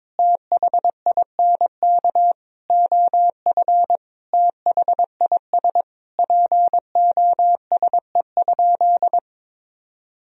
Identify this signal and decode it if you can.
Morse code, heard as a keyed tone, about 22 wpm, 700 Hz. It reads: THINK OF THIS POSE?
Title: Sentences from Top 1000 Words - Review Part 2 - 22wpm